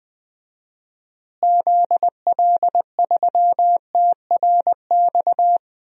Morse code, heard as a keyed tone, about 20 wpm, 700 Hz. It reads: ZL3TRX
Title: DXCC-100 Call Signs - 6 Characters - Repeated 3x - 20wpm